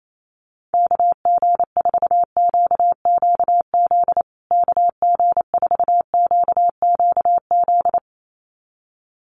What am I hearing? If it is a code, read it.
Morse code, heard as a keyed tone, about 28 wpm, 700 Hz. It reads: KG4QQZ KG4QQZ
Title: USA Call Signs Repeated 2x 28wpm